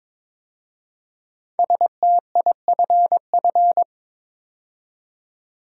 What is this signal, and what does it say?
Morse code, heard as a keyed tone, about 22 wpm, 690 Hz. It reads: STIFF
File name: 5 Letter Words - First 3 Letters Given - 22wpm